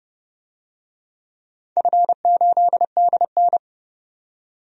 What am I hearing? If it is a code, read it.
Morse code, heard as a keyed tone, about 32 wpm, 700 Hz. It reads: F8DN
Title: DXCC-100 Call Signs - 4 Characters - 30wpm